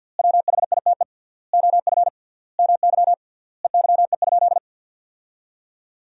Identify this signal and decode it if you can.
Morse code, heard as a keyed tone, about 51 wpm, 700 Hz. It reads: WHITE OF MY EYE?
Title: 3x Word-Spacing - Sentences from Top 500 Words - 50wpm